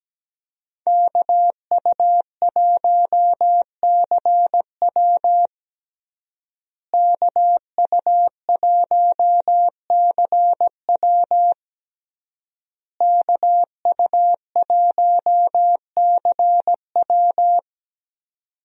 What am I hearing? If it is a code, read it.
Morse code, heard as a keyed tone, about 17 wpm, 700 Hz. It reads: KU1CW KU1CW KU1CW
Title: USA Call Signs - 5 Characters - Repeated 3x - 17wpm